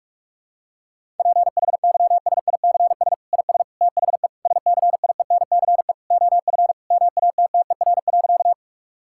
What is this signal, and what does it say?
Morse code, heard as a keyed tone, about 46 wpm, 700 Hz. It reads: PHYSICS IS THE SCIENCE OF MATTER.
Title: Sentences from Top 1000 Words - Review Part 5 - 45wpm